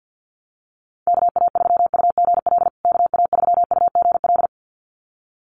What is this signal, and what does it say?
Morse code, heard as a keyed tone, about 50 wpm, 700 Hz. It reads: KA3UGL KA3UGL
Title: USA Call Signs Repeated 2x 50wpm